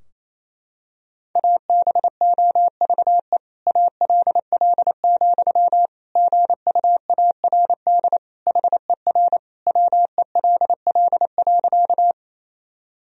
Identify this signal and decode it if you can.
Morse code, heard as a keyed tone, about 27 wpm, 700 Hz. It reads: ABOVE ALL, GUARD HER WELL.
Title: Sentences from Top 1000 Words - Review Part 4 - 28wpm